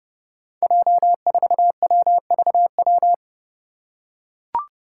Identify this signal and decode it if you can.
Morse code, heard as a keyed tone, about 30 wpm, 700 Hz. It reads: J4WVW
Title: Five Challenging Letter-Number Combinations 30wpm